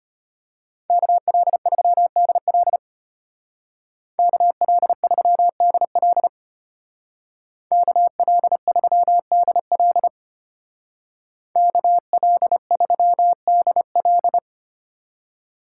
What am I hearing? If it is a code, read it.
Morse code, heard as a keyed tone, about 33 wpm, 700 Hz. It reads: KL3DL KL3DL KL3DL KL3DL